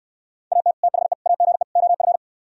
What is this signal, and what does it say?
Morse code, heard as a keyed tone, about 67 wpm, 700 Hz. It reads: AT THE AGE OF